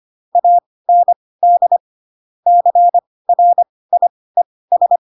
Morse code, heard as a keyed tone, about 25 wpm, spaced as for 18 wpm, 700 Hz. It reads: AND CRIES
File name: Pinocchio / Pinocchio ch7 12 25 ik1hge